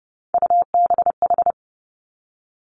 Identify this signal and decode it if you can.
Morse code, heard as a keyed tone, about 30 wpm, 700 Hz. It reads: UBH
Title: Three Challenging Letter-Number Combinations 30wpm